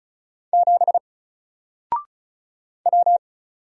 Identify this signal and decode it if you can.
Morse code, heard as a keyed tone, about 36 wpm, 700 Hz. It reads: 7 W
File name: Single-Letter-Number-35wpm